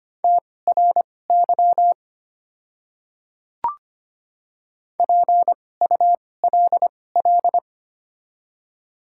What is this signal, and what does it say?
Morse code, heard as a keyed tone, about 25 wpm, spaced as for 18 wpm, 700 Hz. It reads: TRY PULL
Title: 2x Character Spacing - Top 500 Words - 25wpm